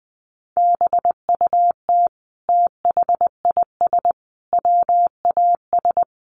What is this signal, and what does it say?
Morse code, heard as a keyed tone, about 20 wpm, 700 Hz. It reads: BUT THIS WAS
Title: Sentences from Top 1500 Words 20wpm